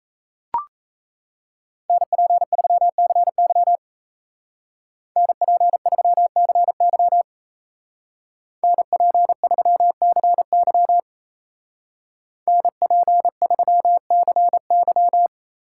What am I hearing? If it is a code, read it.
Morse code, heard as a keyed tone, about 34 wpm, 700 Hz. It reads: NP3CY NP3CY NP3CY NP3CY
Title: Speed Racing - USA 2x2 Call Signs 28wpm